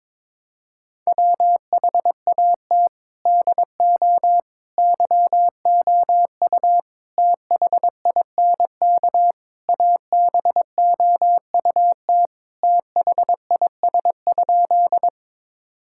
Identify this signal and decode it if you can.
Morse code, heard as a keyed tone, about 21 wpm, 700 Hz. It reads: WHAT DO YOU THINK ABOUT THIS?
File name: practice / 22wpm / Sentences with 4-Word Phrases - 22wpm